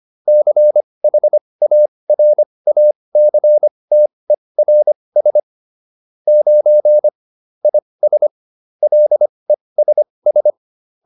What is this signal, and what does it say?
Morse code, heard as a keyed tone, about 25 wpm, spaced as for 20 wpm, 600 Hz. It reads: CHARACTERS 9 IS LESS